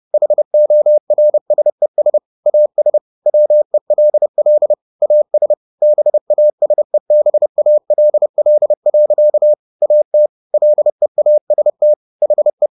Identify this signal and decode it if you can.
Morse code, heard as a keyed tone, about 30 wpm, 600 Hz. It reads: HORSES AS WELL AS BASEBALL. AT LEAST HE
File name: the old man and the sea / the old man and the sea 1 30wpm